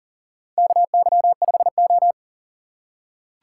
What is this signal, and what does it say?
Morse code, heard as a keyed tone, about 42 wpm, 700 Hz. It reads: KY5O
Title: USA Call Signs - 3x Character Spacing - 40wpm